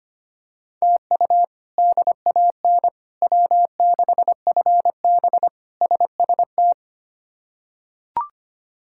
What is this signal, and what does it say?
Morse code, heard as a keyed tone, about 25 wpm, 700 Hz. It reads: TU DAN W6FB SST
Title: K1USN SST QSO Practice - 25wpm